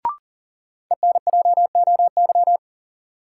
Morse code, heard as a keyed tone, about 40 wpm, 700 Hz. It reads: ENJOY
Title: Sentences from Top 1000 Words - Review Part 4 - New Word Drill - 40wpm